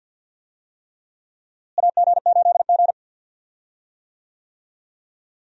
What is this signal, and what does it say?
Morse code, heard as a keyed tone, about 51 wpm, 700 Hz. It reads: AG8G
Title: USA Call Signs - 4 Characters - Repeated 3x - 50wpm